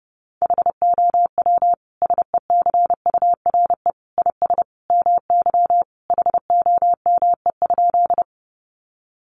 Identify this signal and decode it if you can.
Morse code, heard as a keyed tone, about 28 wpm, 700 Hz. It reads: HOW SECURE IS MY HOME?
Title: Sentences from Top 1200 Words 30wpm